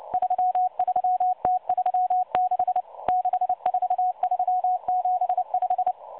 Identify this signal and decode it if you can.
Morse code, heard as a keyed tone, about 29 wpm, 730 Hz. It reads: 33T36 64375
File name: M12 - 8116 kHz CW 2-11-14